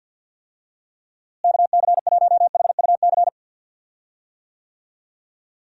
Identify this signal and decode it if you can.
Morse code, heard as a keyed tone, about 53 wpm, 700 Hz. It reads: KC1HUC
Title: USA Call Signs - 6 Characters - Repeated 2x - 50wpm